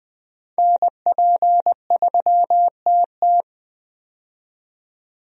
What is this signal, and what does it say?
Morse code, heard as a keyed tone, about 20 wpm, 700 Hz. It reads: NP3TT
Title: USA Call Signs - 5 Characters - 20wpm